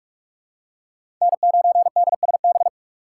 Morse code, heard as a keyed tone, about 46 wpm, 700 Hz. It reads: N9DSB
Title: USA Call Signs - 4x Character Spacing - 45wpm